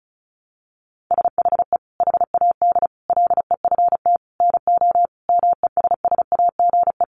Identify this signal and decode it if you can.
Morse code, heard as a keyed tone, about 34 wpm, 700 Hz. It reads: SHE HAD LEFT NO MESSAGE
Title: Sentences from Top 1100 Words 35wpm